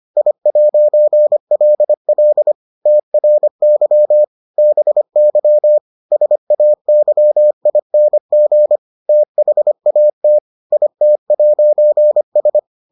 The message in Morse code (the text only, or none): I'LL TRY BY SAYING THAT IT'S